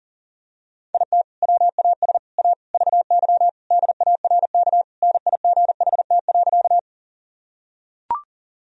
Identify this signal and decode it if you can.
Morse code, heard as a keyed tone, about 40 wpm, 700 Hz. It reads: IT WAS A VY DARK NIGHT.
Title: Sentences from Top 500 Words and Abbreviations 40wpm